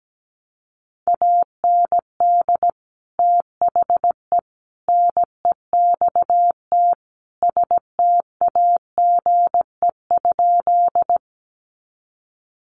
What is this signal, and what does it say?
Morse code, heard as a keyed tone, about 17 wpm, 700 Hz. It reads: AND THE NEXT STAGE?
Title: Sentences from Top 1200 Words 17wpm